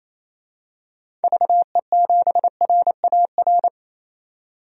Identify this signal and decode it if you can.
Morse code, heard as a keyed tone, about 28 wpm, 700 Hz. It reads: VE7RAR
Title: DXCC-100 Call Signs - 6 Characters - 28wpm